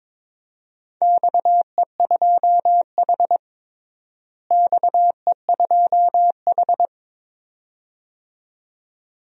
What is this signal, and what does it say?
Morse code, heard as a keyed tone, about 22 wpm, 700 Hz. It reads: XE2H XE2H
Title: DXCC-100 Call Signs - 4 Characters - Repeated 2x - 22wpm